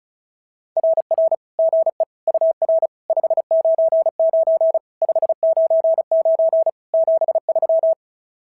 Morse code, heard as a keyed tone, about 36 wpm, 650 Hz. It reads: RR GE UR 599 599 73
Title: SOTA QSO Practice 35wpm